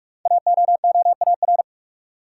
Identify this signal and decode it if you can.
Morse code, heard as a keyed tone, about 44 wpm, 700 Hz. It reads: AOOAR
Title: Code Groups - 5 Letter:Numbers - 45wpm